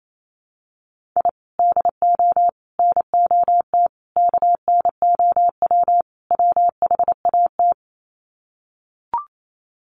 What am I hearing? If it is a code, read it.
Morse code, heard as a keyed tone, about 28 wpm, 700 Hz. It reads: I DO NOT KNOW WHAT
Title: Sets of 5 Words 28wpm